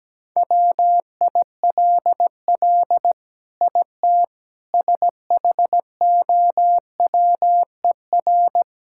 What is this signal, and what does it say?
Morse code, heard as a keyed tone, about 17 wpm, 700 Hz. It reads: WILL IT SHOWER
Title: Sentences from Top 1600 Words 17wpm